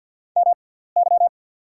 Morse code, heard as a keyed tone, about 48 wpm, spaced as for 17 wpm, 690 Hz. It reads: MY
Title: 6x Character Spacing - Top 500 Words - 50wpm